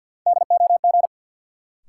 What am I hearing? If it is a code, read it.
Morse code, heard as a keyed tone, about 53 wpm, 700 Hz. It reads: DOG
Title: Sentences from Top 400 Words - New Word Drill - 50wpm